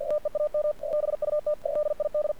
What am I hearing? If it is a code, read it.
Morse code, heard as a keyed tone, about 51 wpm, 600 Hz, with radio noise. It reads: TEAM BUT DID
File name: archived / CRSnet